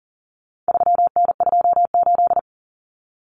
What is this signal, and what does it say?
Morse code, heard as a keyed tone, about 40 wpm, 700 Hz. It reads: 3N28
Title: Common Cut Numbers - 4-Digit 40wpm